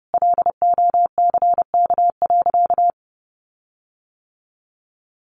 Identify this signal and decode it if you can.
Morse code, heard as a keyed tone, about 30 wpm, 700 Hz. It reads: LOCK.
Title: Sentences from Top 1300 Words 30wpm